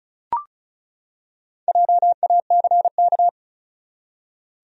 Morse code, heard as a keyed tone, about 36 wpm, 700 Hz. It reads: JACK
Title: USA Common Names - Max 5 Letters - 35wpm